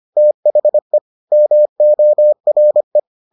Morse code, heard as a keyed tone, about 25 wpm, 600 Hz. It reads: THE MORE